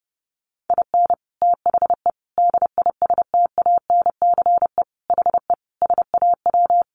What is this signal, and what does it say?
Morse code, heard as a keyed tone, about 28 wpm, 700 Hz. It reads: IN THE DISTANCE HE SAW
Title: Sentences from Top 1200 Words 30wpm